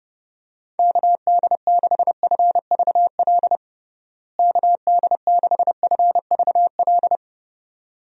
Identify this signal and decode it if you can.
Morse code, heard as a keyed tone, about 32 wpm, 700 Hz. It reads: KD6FVL KD6FVL
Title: USA Call Signs - 6 Characters - Repeated 2x - 30wpm